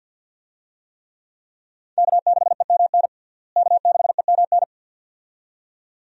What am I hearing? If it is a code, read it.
Morse code, heard as a keyed tone, about 50 wpm, 700 Hz. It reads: K6EMN K6EMN
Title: USA Call Signs - 5 Characters - Repeated 2x - 50wpm